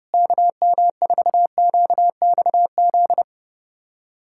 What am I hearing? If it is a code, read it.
Morse code, heard as a keyed tone, about 30 wpm, 700 Hz. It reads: KM4QXZ
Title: USA Callsign DE Callsign - 6 Characters - 30wpm